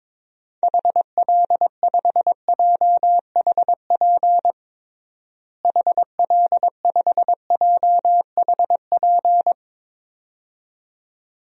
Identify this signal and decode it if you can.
Morse code, heard as a keyed tone, about 22 wpm, 700 Hz. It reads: HL5JHP HL5JHP
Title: DXCC-100 Call Signs - 6 Characters - Repeated 2x - 22wpm